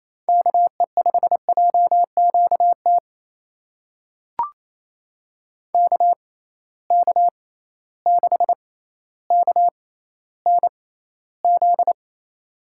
Morse code, heard as a keyed tone, about 28 wpm, spaced as for 14 wpm, 700 Hz. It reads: KE5JQT KK6KNZ
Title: USA Call Signs - 6 Characters - 6x Character Spacing - 28wpm